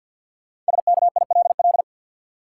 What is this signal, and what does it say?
Morse code, heard as a keyed tone, about 52 wpm, 700 Hz. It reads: SKILL
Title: Sentences from Top 1200 Words - New Word Drill - 50wpm